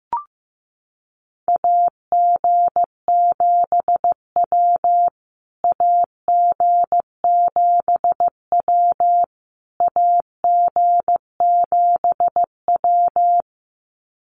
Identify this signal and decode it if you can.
Morse code, heard as a keyed tone, about 15 wpm, 700 Hz. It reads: AG7W AG7W AG7W
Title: USA Call Signs Repeated 3x 15wpm